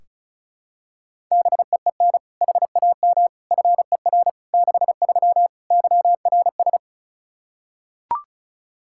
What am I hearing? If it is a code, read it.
Morse code, heard as a keyed tone, about 36 wpm, 700 Hz. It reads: BEEN HAM FER 63 YRS
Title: Practice QSO - Exchange 3 - More Info - 35wpm